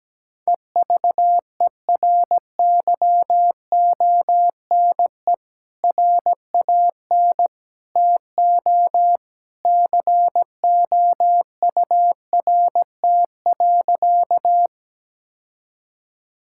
Morse code, heard as a keyed tone, about 17 wpm, 700 Hz. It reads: EVERYONE RAN TO COURT.
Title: Sentences from Top 1600 Words 17wpm